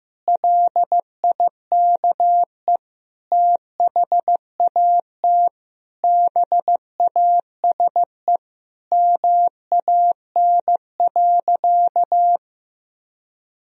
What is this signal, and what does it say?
Morse code, heard as a keyed tone, about 15 wpm, 700 Hz. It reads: LIKE THAT BASE MAN.
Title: Sentences from Top 1000 Words - Review Part 1 - 15wpm